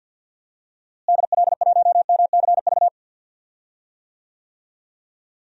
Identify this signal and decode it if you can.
Morse code, heard as a keyed tone, about 46 wpm, 700 Hz. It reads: DL1MCV
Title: DXCC-100 Call Signs - 6 Characters - Repeated 3x - 50wpm